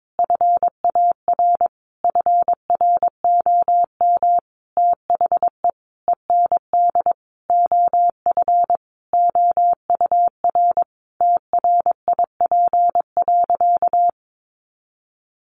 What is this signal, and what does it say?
Morse code, heard as a keyed tone, about 22 wpm, 700 Hz. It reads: FAR FROM THE END OF OUR TRIP.
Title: Sentences from Top 1300 Words 22wpm